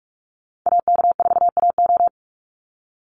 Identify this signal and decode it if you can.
Morse code, heard as a keyed tone, about 44 wpm, 700 Hz. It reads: AK4AO
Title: Call Signs Repeated 3x - 45wpm